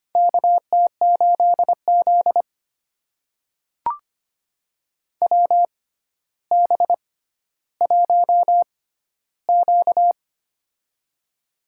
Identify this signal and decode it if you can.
Morse code, heard as a keyed tone, about 25 wpm, spaced as for 13 wpm, 700 Hz. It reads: KT8Z WB1Q
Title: USA Call Signs - 4 Characters - 6x Character Spacing - 25wpm